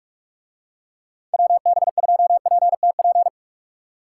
Morse code, heard as a keyed tone, about 46 wpm, 700 Hz. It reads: WB2PTP